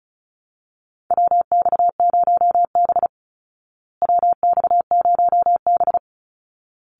Morse code, heard as a keyed tone, about 34 wpm, 700 Hz. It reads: WX0B WX0B
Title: Call Signs Repeated 3x - 35wpm